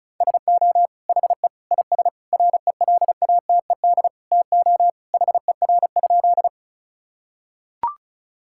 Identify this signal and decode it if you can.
Morse code, heard as a keyed tone, about 36 wpm, 700 Hz. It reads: SO HE IS RELATED TO HER?
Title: Sentences from Top 1000 Words - Review Part 5 - 35wpm